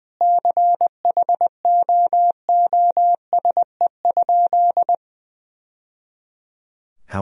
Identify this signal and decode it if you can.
Morse code, heard as a keyed tone, about 21 wpm, 700 Hz. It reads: CHOOSE?
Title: Sentences from Top 1000 Words - Review Part 4 - 20wpm